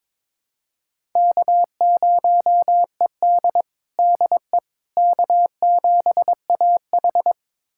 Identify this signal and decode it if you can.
Morse code, heard as a keyed tone, about 23 wpm, 700 Hz. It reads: K0ED DE K7AH
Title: USA Callsign DE Callsign - 4 Characters - 22wpm